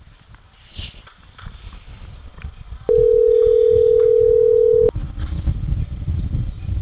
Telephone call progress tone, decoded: ringback tone

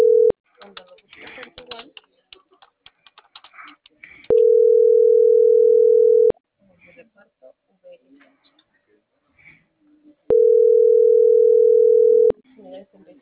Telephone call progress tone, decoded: ringback tone

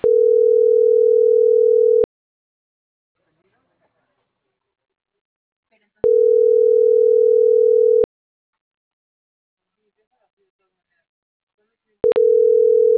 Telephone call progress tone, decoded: ringback tone

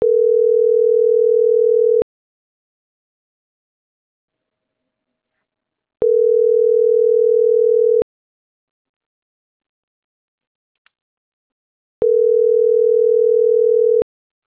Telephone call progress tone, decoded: ringback tone

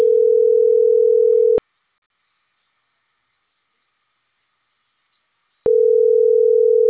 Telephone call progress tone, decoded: ringback tone

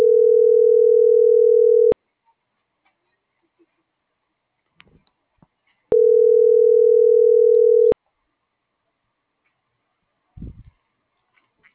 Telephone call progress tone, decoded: ringback tone

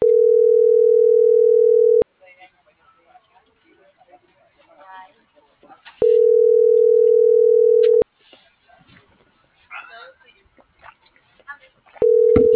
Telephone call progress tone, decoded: ringback tone

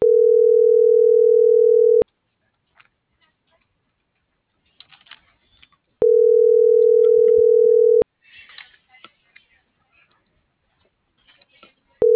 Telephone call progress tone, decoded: ringback tone

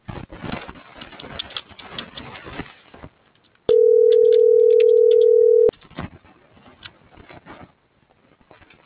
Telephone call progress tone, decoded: ringback tone

